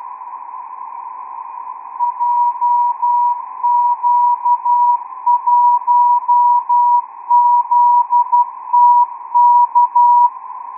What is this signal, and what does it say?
Morse code, heard as a keyed tone, about 12 wpm, 970 Hz. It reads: JQ1ZTK
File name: ITF2-XQUAD-AIRSPY-04